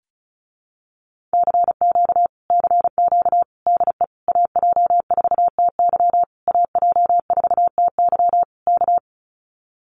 Morse code, heard as a keyed tone, about 36 wpm, 700 Hz. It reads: CQ CQ DE AJ4TY AJ4TY K
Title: Common QSO Phrases Encore 35wpm